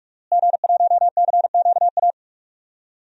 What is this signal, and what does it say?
Morse code, heard as a keyed tone, about 46 wpm, 700 Hz. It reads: G1CQA
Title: DXCC-100 Call Signs - 5 Characters - Repeated 3x - 45wpm